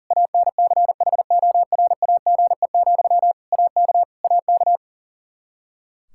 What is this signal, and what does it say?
Morse code, heard as a keyed tone, about 40 wpm, 700 Hz. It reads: ANCHORAGE, AK AK
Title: Practice QSO - Exchange 1 - Beginning - 40wpm